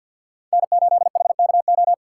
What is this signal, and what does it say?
Morse code, heard as a keyed tone, about 63 wpm, 700 Hz. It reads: N8HKO